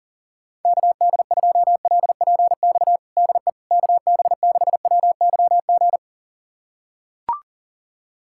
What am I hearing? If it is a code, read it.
Morse code, heard as a keyed tone, about 40 wpm, 700 Hz. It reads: KD2LPX DE KB6WYG
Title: USA Callsign DE Callsign - 6 Characters - 40wpm